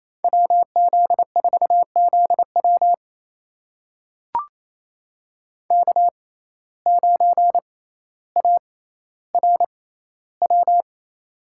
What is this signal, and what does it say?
Morse code, heard as a keyed tone, about 28 wpm, spaced as for 14 wpm, 700 Hz. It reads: WZ4ZW K9ARW
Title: USA Call Signs - 5 Characters - 6x Character Spacing - 28wpm